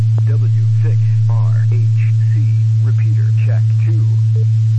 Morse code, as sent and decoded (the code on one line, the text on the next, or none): none